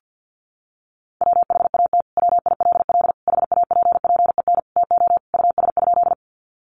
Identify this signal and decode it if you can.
Morse code, heard as a keyed tone, about 50 wpm, 700 Hz. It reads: WHAT WILL HAPPEN TO US?
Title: Sentences from Top 400 Words 50wpm